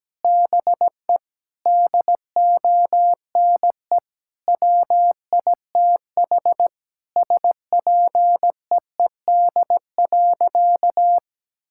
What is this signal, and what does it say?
Morse code, heard as a keyed tone, about 17 wpm, 700 Hz. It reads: BE DONE WITH SPEED.
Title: Sentences from Top 1000 Words - Review Part 5 - 17wpm